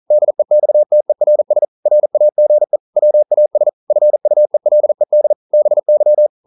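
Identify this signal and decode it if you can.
Morse code, heard as a keyed tone, about 42 wpm, 600 Hz. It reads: DEXTERS RAGE WAS FUELED BY
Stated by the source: Morse code practice at 41 wpm, from CW med Gnister.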